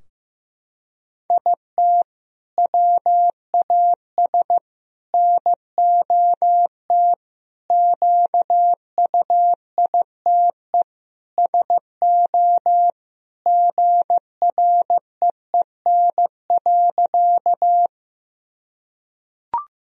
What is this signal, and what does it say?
Morse code, heard as a keyed tone, about 15 wpm, 700 Hz. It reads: IT WAS NOT QUITE SO GREEN.
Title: Sentences from Top 1000 Words - Review Part 2 - 15wpm